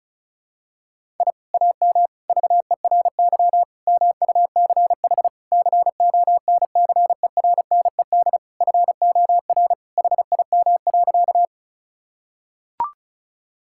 Morse code, heard as a keyed tone, about 34 wpm, 700 Hz. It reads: I AM VERY MUCH CONCERNED FOR HIM.
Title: Sentences from Top 1000 Words - Review Part 6 - 35wpm